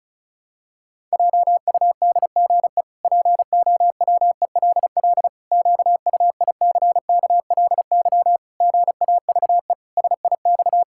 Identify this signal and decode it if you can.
Morse code, heard as a keyed tone, about 36 wpm, 700 Hz. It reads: JUDGE POWELL QUICKLY GAVE SIX
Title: Pangram Challenge v2 - 35wpm